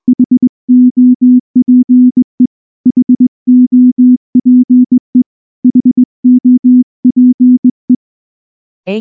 Morse code, heard as a keyed tone, about 20 wpm, 250 Hz. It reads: HOPE HOPE HOPE